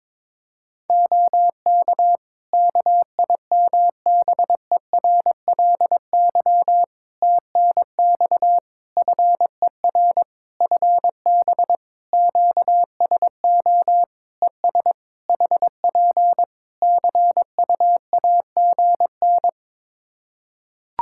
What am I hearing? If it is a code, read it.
Morse code, heard as a keyed tone, about 22 wpm, 700 Hz. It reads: OK KIMBERLY TNX FER FB QSO ES HP CUAGN
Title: Practice QSO - Exchange 4 - Ending - 22wpm